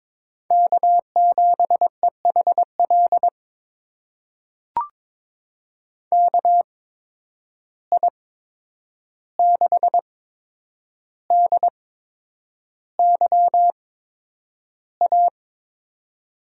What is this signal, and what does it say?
Morse code, heard as a keyed tone, about 22 wpm, spaced as for 9 wpm, 700 Hz. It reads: K7EHL KI6DYA
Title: USA Call Signs - 8x Character Spacing - 22wpm